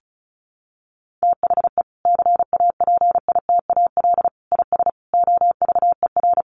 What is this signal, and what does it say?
Morse code, heard as a keyed tone, about 34 wpm, 700 Hz. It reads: THE CAPITAL IS OVER
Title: Sentences from Top 1000 Words 35wpm